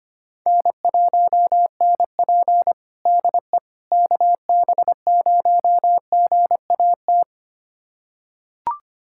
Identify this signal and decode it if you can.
Morse code, heard as a keyed tone, about 25 wpm, 700 Hz. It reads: N1NP DE KB0GAT
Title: USA Callsign DE Callsign 25wpm